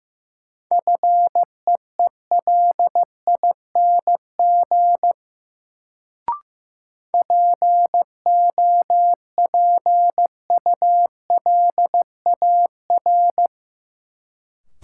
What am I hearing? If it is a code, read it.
Morse code, heard as a keyed tone, about 15 wpm, 700 Hz. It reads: FEELING POPULAR
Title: Sentences from Top 1000 Words - Review Part 3 - New Word Drill - 15wpm